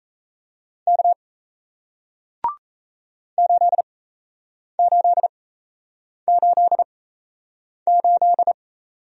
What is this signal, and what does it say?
Morse code, heard as a keyed tone, about 37 wpm, spaced as for 8 wpm, 700 Hz. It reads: K 8888